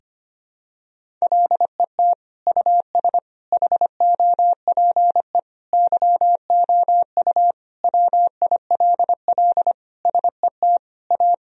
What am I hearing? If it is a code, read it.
Morse code, heard as a keyed tone, about 25 wpm, 700 Hz. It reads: LET US HOPE YOU WILL SET A
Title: Sentences from Top 1000 Words - Review Part 1 - 25wpm